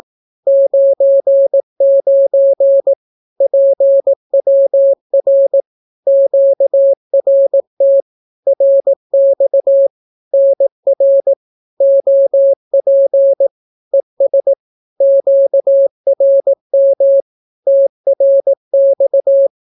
Morse code, heard as a keyed tone, about 18 wpm, 550 Hz. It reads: 99 PWR QRT RX NR OP ES QRM TRX
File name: naukapodstaw / skroty / skroty